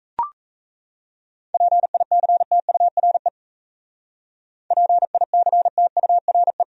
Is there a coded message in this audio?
Yes: Morse code from 1.5 to 6.6 s, heard as a keyed tone, about 42 wpm, 700 Hz. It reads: PICTURE PICTURE